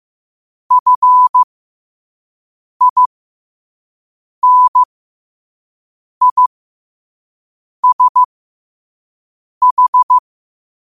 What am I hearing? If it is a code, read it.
Morse code, heard as a keyed tone, about 15 wpm, spaced as for 5 wpm, 1000 Hz. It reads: FINISH
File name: qotdfiles / 5wpm